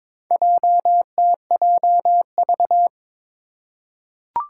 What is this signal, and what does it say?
Morse code, heard as a keyed tone, about 21 wpm, 700 Hz. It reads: JTJV
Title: Four Challenging Letter-Number Combinations 22wpm